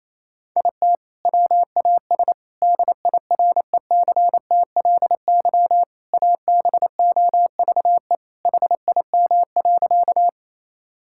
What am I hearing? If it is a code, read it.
Morse code, heard as a keyed tone, about 28 wpm, 700 Hz. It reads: IT WAS DIRECTLY ABOVE HIM.
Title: Sentences from Top 1000 Words - Review Part 5 - 28wpm